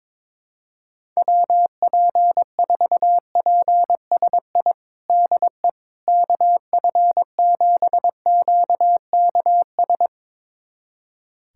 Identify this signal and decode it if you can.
Morse code, heard as a keyed tone, about 22 wpm, 700 Hz. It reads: WP4PSI DE KF7QKS
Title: USA Callsign DE Callsign - 6 Characters - 22wpm